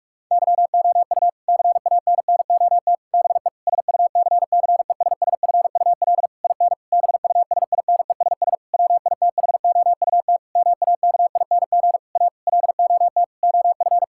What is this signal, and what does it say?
Morse code, heard as a keyed tone, about 41 wpm, 700 Hz. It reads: YOU CANNOT BE SUCCESSFUL IN BUSINESS WITHOUT MAKING A LOT OF
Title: Sentences from Top 1000 Words - Review Part 6 - 45wpm